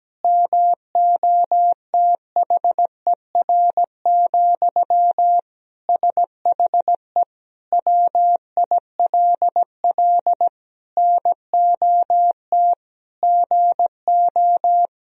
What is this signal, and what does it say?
Morse code, heard as a keyed tone, about 17 wpm, 700 Hz. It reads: MOTHER, SHE WILL NOT GO
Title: Sentences from Top 1000 Words - Review Part 4 - 17wpm